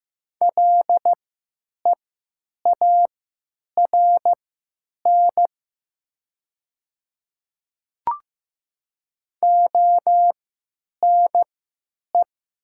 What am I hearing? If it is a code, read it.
Morse code, heard as a keyed tone, about 15 wpm, spaced as for 9 wpm, 700 Hz. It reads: LEARN ONE
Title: 3x Character Spacing - Top 300 Words - 15wpm